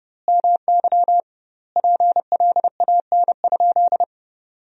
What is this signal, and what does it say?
Morse code, heard as a keyed tone, about 30 wpm, 700 Hz. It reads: MY PLAN?
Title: 2x Word-Spacing - Sentences from Top 500 Words - 30wpm